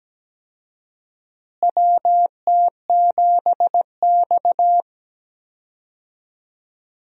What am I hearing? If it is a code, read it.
Morse code, heard as a keyed tone, about 17 wpm, 700 Hz. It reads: WT7X
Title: USA Call Signs - 4 Characters - Repeated 3x - 17wpm